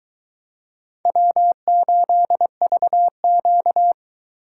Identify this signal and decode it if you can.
Morse code, heard as a keyed tone, about 22 wpm, 700 Hz. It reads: W8VQ